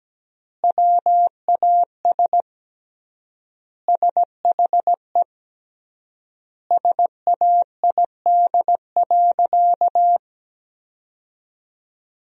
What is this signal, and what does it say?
Morse code, heard as a keyed tone, about 17 wpm, 700 Hz. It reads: WAS SHE SAID.
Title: 3x Word-Spacing - Sentences from Top 400 Words - 17wpm